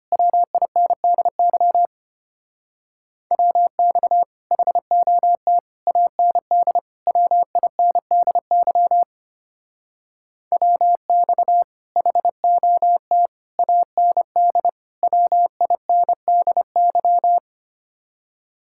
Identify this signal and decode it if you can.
Morse code, heard as a keyed tone, about 30 wpm, 700 Hz. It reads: WINDY WX HOT AND WINDY WX HOT AND WINDY
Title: Speed-Racing Common QSO Phrases Encore - 25wpm